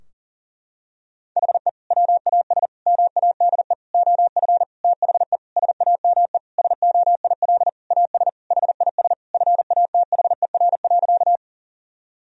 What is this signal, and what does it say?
Morse code, heard as a keyed tone, about 40 wpm, 700 Hz. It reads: HE WAS MADE OF THE SAME SOIL AS HIS FATHER.
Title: Sentences from Top 1000 Words - Review Part 4 - 40wpm